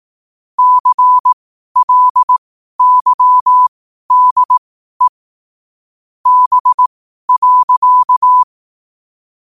Morse code, heard as a keyed tone, about 18 wpm, spaced as for 12 wpm, 1000 Hz. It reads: CLYDE B.
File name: qotdfiles / 12wpm